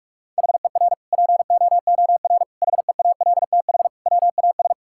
Morse code, heard as a keyed tone, about 47 wpm, 700 Hz. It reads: HER POOR HEALTH WAS
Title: Sentences from Top 1000 Words - Review Part 6 - 45wpm